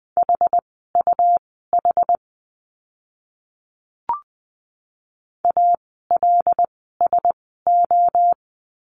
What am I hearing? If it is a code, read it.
Morse code, heard as a keyed tone, about 20 wpm, spaced as for 14 wpm, 700 Hz. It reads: HUH ALSO
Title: 101 Lesson 17f - Letter u - Words 20wpm